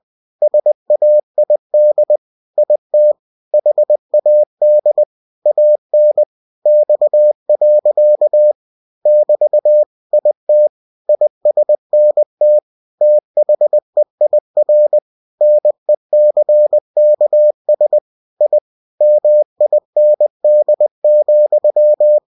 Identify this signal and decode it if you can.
Morse code, heard as a keyed tone, about 20 wpm, 600 Hz. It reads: SAID IT HAD AN X. <BT> IT ISNT THEIR NECKS I MIND,
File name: cw-books / winnie-the-pooh / winnie-20wpm-page-0079